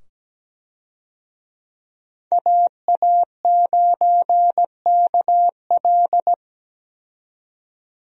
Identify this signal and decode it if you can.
Morse code, heard as a keyed tone, about 17 wpm, 700 Hz. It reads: AA9KL
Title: USA Call Signs - 5 Characters - 17wpm